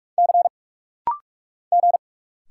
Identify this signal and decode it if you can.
Morse code, heard as a keyed tone, about 44 wpm, 700 Hz. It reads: C G
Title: Single Letter-Number - Rapid-Fire 45wpm